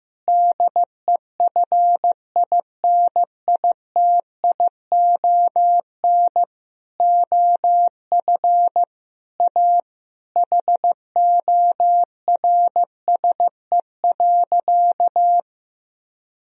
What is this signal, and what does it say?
Morse code, heard as a keyed tone, about 15 wpm, 700 Hz. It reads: DEFINITION OF A HORSE.
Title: Sentences from Top 1000 Words - Review Part 4 - 15wpm